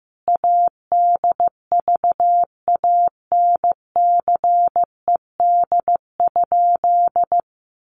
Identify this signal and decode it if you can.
Morse code, heard as a keyed tone, about 15 wpm, 700 Hz. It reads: ADVANCED?
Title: Sentences from Top 1200 Words 15wpm